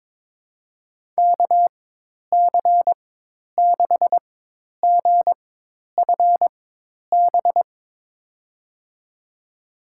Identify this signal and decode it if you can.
Morse code, heard as a keyed tone, about 22 wpm, spaced as for 10 wpm, 700 Hz. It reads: KC6GFB
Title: USA Call Signs - 4x Character Spacing - 22wpm